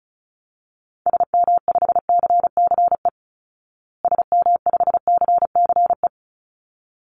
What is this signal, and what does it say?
Morse code, heard as a keyed tone, about 34 wpm, 700 Hz. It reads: SM5CCE SM5CCE
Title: Call Signs Repeated 3x - 35wpm